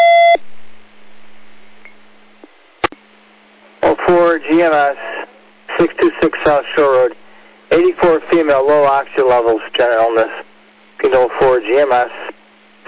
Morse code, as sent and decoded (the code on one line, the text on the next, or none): none